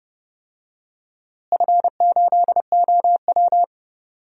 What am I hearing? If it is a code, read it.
Morse code, heard as a keyed tone, about 31 wpm, 700 Hz. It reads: F8OW
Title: DXCC-100 Call Signs - 4 Characters - Repeated 3x - 30wpm